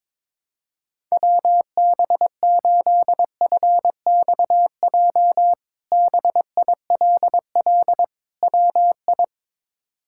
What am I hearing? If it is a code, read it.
Morse code, heard as a keyed tone, about 22 wpm, 700 Hz. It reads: WB8FXJ BILL WI
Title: K1USN SST QSO Practice - 22wpm